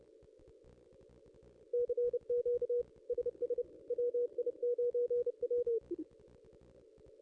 Morse code, heard as a keyed tone, about 32 wpm, 480 Hz. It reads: CQ SS WI9W
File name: contests / ss-cw-k5tr-op-n5rz -1700